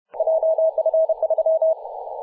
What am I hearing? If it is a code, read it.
Morse code, heard as a keyed tone, about 30 wpm, 640 Hz. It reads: JF3
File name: posts / 2017-apr-02-0450z-3d2ag-p-18-087mhz-cw